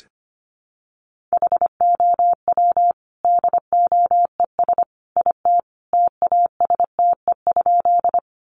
Morse code, heard as a keyed tone, about 25 wpm, 700 Hz. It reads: HOW DOES IT TASTE?
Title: Sentences from Top 1100 Words 25wpm